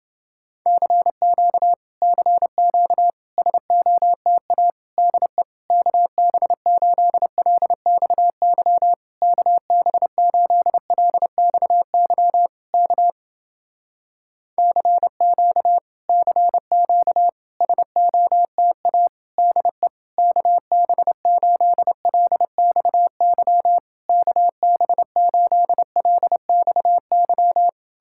Morse code, heard as a keyed tone, about 28 wpm, 700 Hz. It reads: CQ CQ SOTA DE KB8LXY KB8LXY K CQ CQ SOTA DE KB8LXY KB8LXY